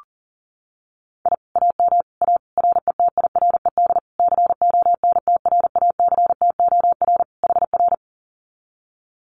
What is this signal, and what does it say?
Morse code, heard as a keyed tone, about 40 wpm, 700 Hz. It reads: I AM A RETIRED CONTRACTOR HR
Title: Common QSO Phrases Encore 40wpm